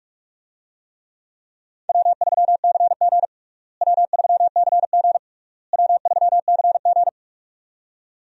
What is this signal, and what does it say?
Morse code, heard as a keyed tone, about 41 wpm, 700 Hz. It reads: W3CG W3CG W3CG
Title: USA Call Signs - 4 Characters - Repeated 3x - 45wpm